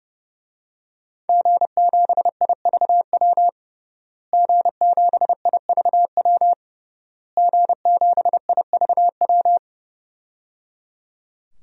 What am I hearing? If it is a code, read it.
Morse code, heard as a keyed tone, about 29 wpm, 700 Hz. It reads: G7IVW G7IVW G7IVW
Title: DXCC-100 Call Signs - 5 Characters - Repeated 3x - 30wpm